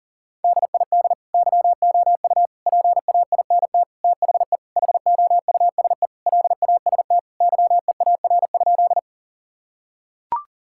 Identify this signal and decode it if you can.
Morse code, heard as a keyed tone, about 37 wpm, 700 Hz. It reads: DID YOU PAINT THE HOUSE LAST YEAR?
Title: Sentences from Top 1000 Words - Review Part 6 - 40wpm